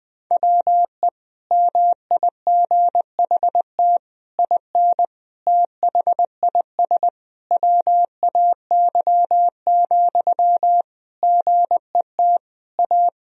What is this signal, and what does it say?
Morse code, heard as a keyed tone, about 20 wpm, 700 Hz. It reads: WE MIGHT IN THIS WAY, GET A
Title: Sentences from Top 1000 Words - Review Part 1 - 20wpm